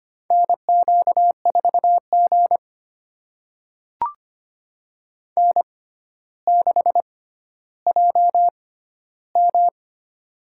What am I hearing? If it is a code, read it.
Morse code, heard as a keyed tone, about 25 wpm, spaced as for 13 wpm, 700 Hz. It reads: NQ4G N6JM
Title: USA Call Signs - 4 Characters - 6x Character Spacing - 25wpm